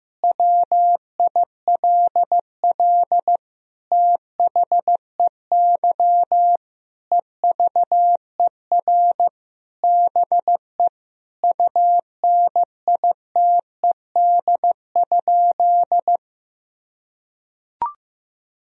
Morse code, heard as a keyed tone, about 16 wpm, 700 Hz. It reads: WILL THEY EVER BE UNITED?
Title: Sentences from Top 1000 Words - Review Part 4 - 15wpm